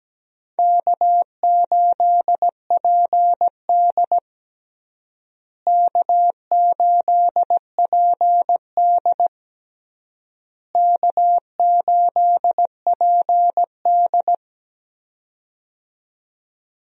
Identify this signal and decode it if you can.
Morse code, heard as a keyed tone, about 17 wpm, 700 Hz. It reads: K8PD K8PD K8PD
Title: USA Call Signs - 4 Characters - Repeated 3x - 17wpm